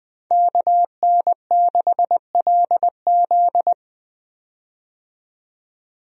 Morse code, heard as a keyed tone, about 20 wpm, 700 Hz. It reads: KN6LZ
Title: USA Call Signs - 5 Characters - 20wpm